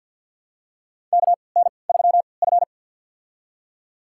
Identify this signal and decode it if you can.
Morse code, heard as a keyed tone, about 46 wpm, spaced as for 27 wpm, 700 Hz. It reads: KN3F
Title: USA Call Signs - 4 Characters - 3x Character Spacing - 50wpm